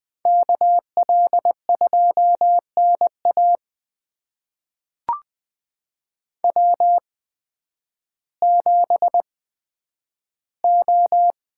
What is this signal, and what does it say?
Morse code, heard as a keyed tone, about 20 wpm, spaced as for 11 wpm, 700 Hz. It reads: KL2NA W7O
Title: USA Call Signs - 8x Character Spacing - 20wpm